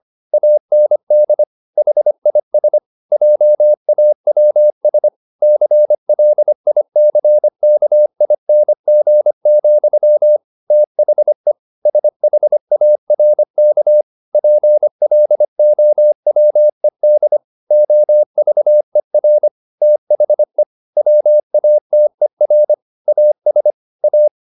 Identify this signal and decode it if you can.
Morse code, heard as a keyed tone, about 25 wpm, 600 Hz. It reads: AND HIS JAWS CLICKING, THE SHARK PLOWED OVER THE WATER AS A